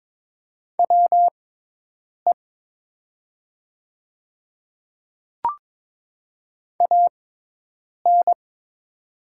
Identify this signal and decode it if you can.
Morse code, heard as a keyed tone, about 22 wpm, spaced as for 8 wpm, 700 Hz. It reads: WE AN
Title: 6x Character Spacing - Top 500 Words - 22wpm